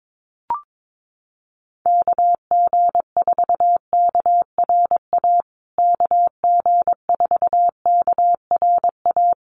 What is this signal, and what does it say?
Morse code, heard as a keyed tone, about 22 wpm, 700 Hz. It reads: KG4KRA KG4KRA
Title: USA Call Signs Repeated 2x 22wpm